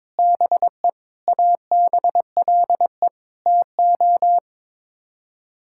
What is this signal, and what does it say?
Morse code, heard as a keyed tone, about 22 wpm, 700 Hz. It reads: BE ABLE TO
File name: practice / 22wpm / Sets of 6 Words - 22wpm